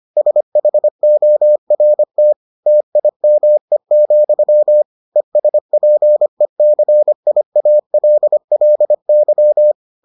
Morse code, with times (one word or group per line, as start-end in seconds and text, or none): 0.2-2.3 SHORT
2.7-4.8 TIME,
5.2-9.7 ESPECIALLY